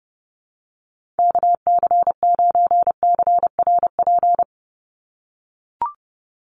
Morse code, heard as a keyed tone, about 30 wpm, 700 Hz. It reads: KC9CRP
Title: USA Call Signs 30wpm